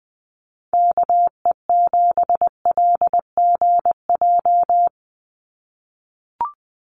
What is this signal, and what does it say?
Morse code, heard as a keyed tone, about 20 wpm, 700 Hz. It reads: KE7LGJ
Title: USA Call Signs 20wpm